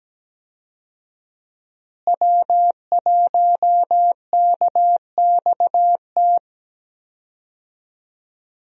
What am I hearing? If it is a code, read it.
Morse code, heard as a keyed tone, about 17 wpm, 700 Hz. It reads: W1KXT
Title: USA Call Signs - 5 Characters - Repeated 2x - 17wpm